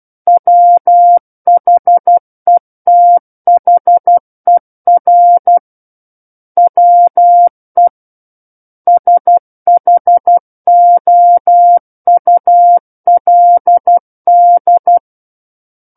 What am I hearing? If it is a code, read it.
Morse code, heard as a keyed tone, about 12 wpm, 700 Hz. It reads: WHETHER WE SHOULD
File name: Book One / 03-Chapter 3 - 12 WPM